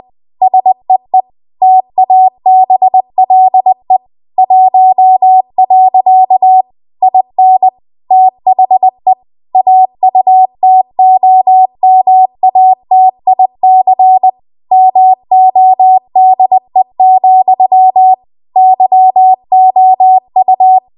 Morse code, heard as a keyed tone, about 20 wpm, 750 Hz. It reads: SEE TABLE 1. IN THE AUTOMATIC MODE, YOU
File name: Archive / 20 WPM / 20WPM